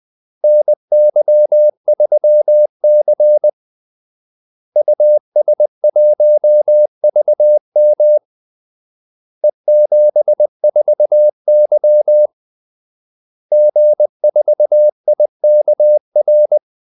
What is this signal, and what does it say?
Morse code, heard as a keyed tone, about 20 wpm, 600 Hz. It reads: NY3C US1VM E74Y G4IKR
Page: Kaldesignaler 20 wpm